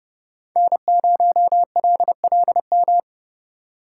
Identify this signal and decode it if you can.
Morse code, heard as a keyed tone, about 26 wpm, 700 Hz. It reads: N0LLM